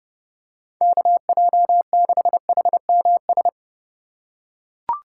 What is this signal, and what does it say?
Morse code, heard as a keyed tone, about 30 wpm, 700 Hz. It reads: KJ6HMS